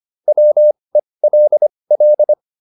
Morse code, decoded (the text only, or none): WELL